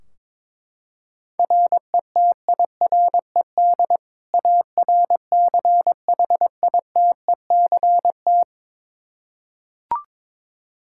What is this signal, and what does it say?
Morse code, heard as a keyed tone, about 22 wpm, 700 Hz. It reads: RETIRED ARCHITECT
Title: Practice QSO - Exchange 3 - More Info - 22wpm